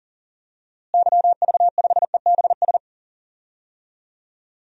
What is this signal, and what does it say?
Morse code, heard as a keyed tone, about 40 wpm, 700 Hz. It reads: YV5EBS
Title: DXCC-100 Call Signs - 6 Characters - Repeated 2x - 40wpm